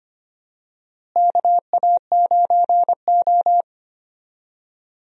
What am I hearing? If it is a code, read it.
Morse code, heard as a keyed tone, about 25 wpm, 700 Hz. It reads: KA9O
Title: USA Call Signs - 4 Characters - Repeated 2x - 25wpm